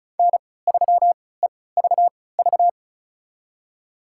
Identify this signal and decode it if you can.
Morse code, heard as a keyed tone, about 36 wpm, spaced as for 20 wpm, 700 Hz. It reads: N3EVV
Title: USA Call Signs - 5 Characters - 3x Character Spacing - 35wpm